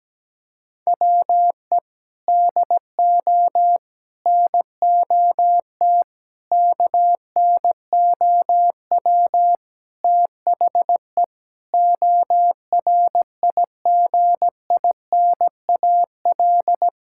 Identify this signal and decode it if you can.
Morse code, heard as a keyed tone, about 17 wpm, 700 Hz. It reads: WE DO NOT KNOW THE ORIGINAL
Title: Sentences from Top 1000 Words - Review Part 6 - 17wpm